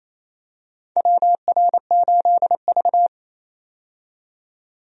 Morse code, heard as a keyed tone, about 27 wpm, 700 Hz. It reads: WR8V
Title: USA Call Signs - 4 Characters - 28wpm